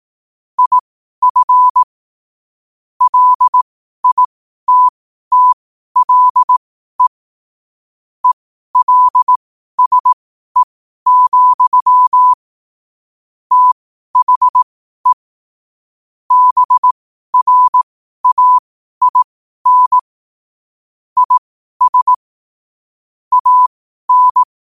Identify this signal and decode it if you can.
Morse code, heard as a keyed tone, about 18 wpm, spaced as for 12 wpm, 1000 Hz. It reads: IF LITTLE ELSE, THE BRAIN IS AN
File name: qotdfiles / 12wpm